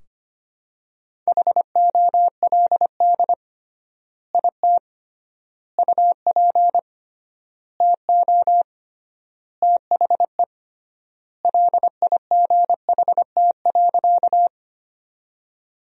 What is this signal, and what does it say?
Morse code, heard as a keyed tone, about 25 wpm, 700 Hz. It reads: HOLD IT UP TO THE LIGHT.
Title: 3x Word-Spacing - Sentences from Top 400 Words - 25wpm